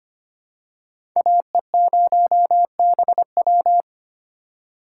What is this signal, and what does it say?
Morse code, heard as a keyed tone, about 25 wpm, 700 Hz. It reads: AE0BW
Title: USA Call Signs - 3x Character Spacing - 25wpm